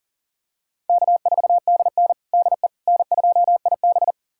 Morse code, heard as a keyed tone, about 40 wpm, 700 Hz. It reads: K4DN DE N2IB
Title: USA Callsign DE Callsign - 4 Characters - 40wpm